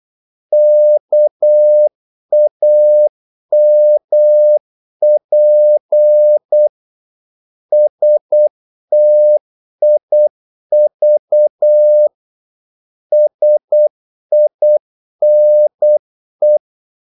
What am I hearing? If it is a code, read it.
Morse code, heard as a keyed tone, about 8 wpm, 600 Hz. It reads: KAMP STIV SINE
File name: Korte ord DK 8wpm